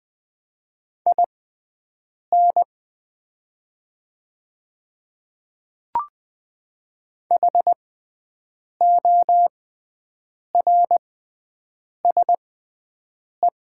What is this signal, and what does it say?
Morse code, heard as a keyed tone, about 20 wpm, spaced as for 7 wpm, 700 Hz. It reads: IN HORSE
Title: 6x Character Spacing - Top 300 Words - 20wpm